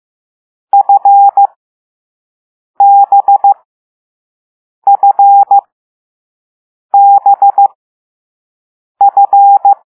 Morse code, heard as a keyed tone, about 15 wpm, spaced as for 5 wpm, 800 Hz. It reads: FBFBF
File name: Morse / Level6B